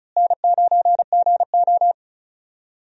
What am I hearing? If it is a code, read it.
Morse code, heard as a keyed tone, about 34 wpm, 700 Hz. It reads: N9GO
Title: USA Call Signs - 4 Characters - Repeated 3x - 35wpm